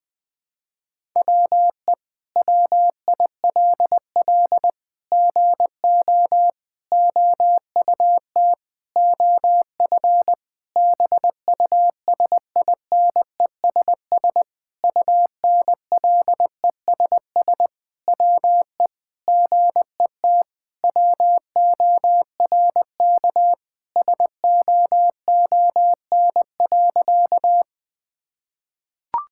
Morse code, heard as a keyed tone, about 20 wpm, 700 Hz. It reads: WE WILL GO OUT OF BUSINESS UNLESS WE GET WORK SOON.
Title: Sentences from Top 1000 Words - Review Part 6 - 20wpm